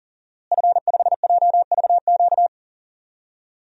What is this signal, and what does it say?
Morse code, heard as a keyed tone, about 40 wpm, 700 Hz. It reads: F5JVQ
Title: DXCC-100 Call Signs - 5 Characters - Repeated 2x - 40wpm